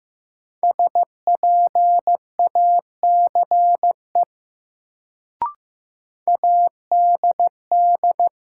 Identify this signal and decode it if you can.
Morse code, heard as a keyed tone, about 16 wpm, 700 Hz. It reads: SPACE ADD
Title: Sentences from Top 1500 Words - New Word Drill Alt - 15wpm